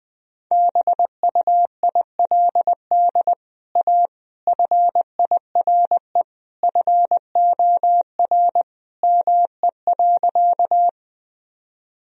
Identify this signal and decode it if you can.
Morse code, heard as a keyed tone, about 21 wpm, 700 Hz. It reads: BUILD A FIRE FOR ME.
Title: Sentences from Top 1000 Words - Review Part 1 - 20wpm